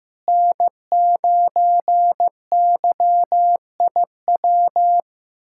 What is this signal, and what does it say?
Morse code, heard as a keyed tone, about 15 wpm, 700 Hz. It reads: N9YIW
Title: USA Callsign DE Callsign - 5 Characters - 15wpm